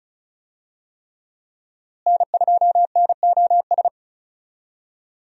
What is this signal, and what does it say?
Morse code, heard as a keyed tone, about 36 wpm, 700 Hz. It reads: N2NOS
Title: USA Call Signs - 5 Characters - Repeated 3x - 35wpm